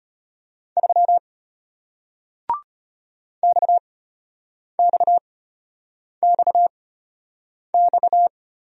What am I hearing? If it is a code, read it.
Morse code, heard as a keyed tone, about 32 wpm, spaced as for 8 wpm, 690 Hz. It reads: 3 XXXX